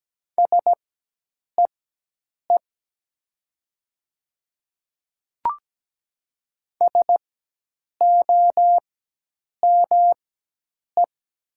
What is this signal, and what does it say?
Morse code, heard as a keyed tone, about 17 wpm, spaced as for 8 wpm, 700 Hz. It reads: SEE SOME
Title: 4x Character Spacing - Top 100 Words - 17wpm